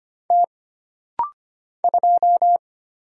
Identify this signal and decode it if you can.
Morse code, heard as a keyed tone, about 25 wpm, 700 Hz. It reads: T 2
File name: Single Character (Letter:Number:Punctuation:Prosign) - Rapid-Fire - 25wpm